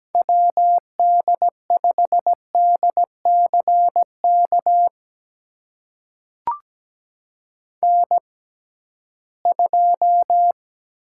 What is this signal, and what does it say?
Morse code, heard as a keyed tone, about 17 wpm, spaced as for 13 wpm, 700 Hz. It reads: WD5DCK N2
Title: USA Call Signs - 6x Character Spacing - 17wpm